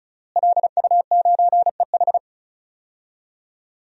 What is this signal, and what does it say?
Morse code, heard as a keyed tone, about 34 wpm, 700 Hz. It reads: LU9EH
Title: DXCC-100 Call Signs - 5 Characters - Repeated 3x - 35wpm